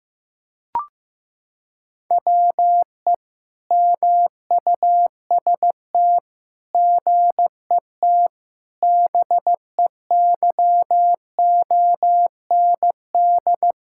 Morse code, heard as a keyed tone, about 16 wpm, 700 Hz. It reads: WE MUST GET BEYOND
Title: Sentences from Top 1000 Words - Review Part 6 - 15wpm